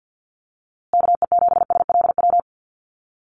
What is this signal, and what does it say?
Morse code, heard as a keyed tone, about 50 wpm, 700 Hz. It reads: KE7SLW
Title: USA Call Signs Repeated 2x 50wpm